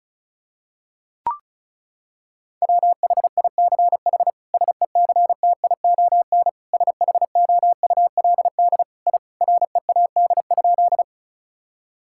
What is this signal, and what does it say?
Morse code, heard as a keyed tone, about 36 wpm, 700 Hz. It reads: WHICH SECTION SHOULD I READ?
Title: Sentences from Top 1000 Words - Review Part 5 - 35wpm